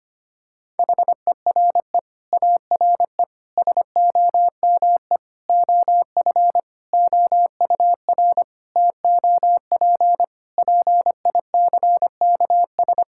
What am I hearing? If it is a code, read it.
Morse code, heard as a keyed tone, about 25 wpm, 700 Hz. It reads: HERE ARE SOME OF OUR TOP PICKS
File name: practice / 25wpm / Sentences with 9-Word Phrases - 25wpm